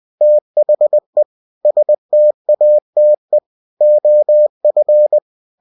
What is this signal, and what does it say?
Morse code, heard as a keyed tone, about 20 wpm, 600 Hz. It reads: THE STATE OF